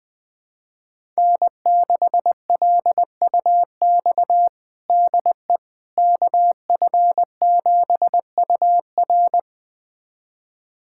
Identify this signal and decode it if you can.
Morse code, heard as a keyed tone, about 20 wpm, 700 Hz. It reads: N6LUX DE KF7UR
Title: USA Callsign DE Callsign - 5 Characters - 20wpm